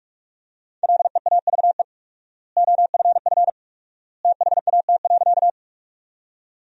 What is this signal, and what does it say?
Morse code, heard as a keyed tone, about 46 wpm, 700 Hz. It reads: LEAVE OFF THAT.
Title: 4x Word-Spacing - Sentences from Top 400 Words - 45wpm